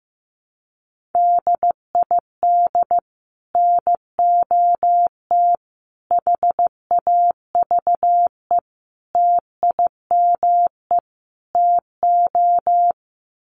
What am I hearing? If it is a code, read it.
Morse code, heard as a keyed tone, about 15 wpm, 700 Hz. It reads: DID NOT HAVE TIME TO
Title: Sets of 5 Words Encore 15wpm